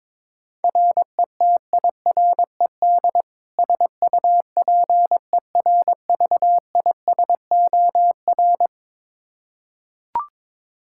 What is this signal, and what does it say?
Morse code, heard as a keyed tone, about 22 wpm, 700 Hz. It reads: RETIRED SUPERVISOR
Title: Practice QSO - Exchange 3 - More Info - 22wpm